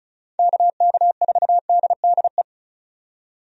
Morse code, heard as a keyed tone, about 36 wpm, 700 Hz. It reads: KK4DDE
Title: USA Call Signs - 6 Characters - 35wpm